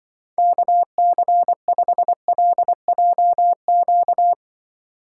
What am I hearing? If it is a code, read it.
Morse code, heard as a keyed tone, about 24 wpm, 700 Hz. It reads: KC5LJQ
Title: Speed Racing - USA 2x3 Call Signs 20wpm